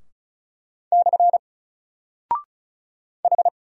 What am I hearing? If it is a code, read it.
Morse code, heard as a keyed tone, about 36 wpm, 700 Hz. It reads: / H
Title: Single Character (Letter:Number:Punctuation:Prosign) Alt 35wpm